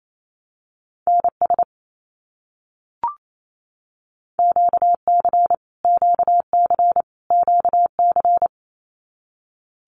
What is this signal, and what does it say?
Morse code, heard as a keyed tone, about 28 wpm, 700 Hz. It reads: NS QC QC QC
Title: Canadian Provinces and Territories - Repeated 3x - 28wpm